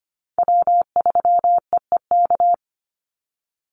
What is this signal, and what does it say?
Morse code, heard as a keyed tone, about 25 wpm, 700 Hz. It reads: W3EEK
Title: Call Signs - 25wpm